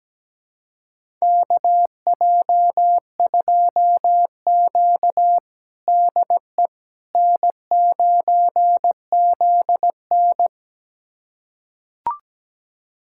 Morse code, heard as a keyed tone, about 17 wpm, 700 Hz. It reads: KJ2Q DE N9ZN
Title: USA Callsign DE Callsign - 4 Characters - 17wpm